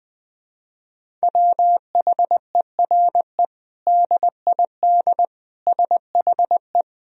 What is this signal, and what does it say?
Morse code, heard as a keyed tone, about 20 wpm, 700 Hz. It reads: WHERE DID SHE
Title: Sentences from Top 200 Words and Abbreviations 20wpm